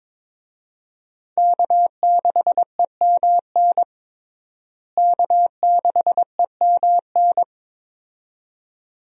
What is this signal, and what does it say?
Morse code, heard as a keyed tone, about 22 wpm, 700 Hz. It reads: K6EMN K6EMN
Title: USA Call Signs - 5 Characters - Repeated 2x - 22wpm